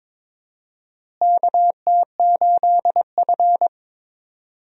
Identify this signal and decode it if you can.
Morse code, heard as a keyed tone, about 22 wpm, 700 Hz. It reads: KT8F
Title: USA Call Signs - 4 Characters - Repeated 3x - 22wpm